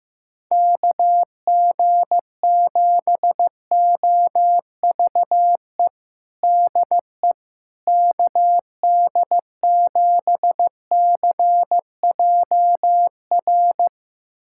Morse code, heard as a keyed tone, about 15 wpm, 700 Hz. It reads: KG7OVE DE KD7CJR
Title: USA Callsign DE Callsign 15wpm